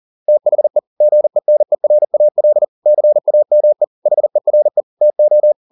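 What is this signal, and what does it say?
Morse code, heard as a keyed tone, about 40 wpm, 600 Hz. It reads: THE GENERAL CAME HERE TO
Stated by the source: Morse code at 40 wpm, practice text, CW med Gnister.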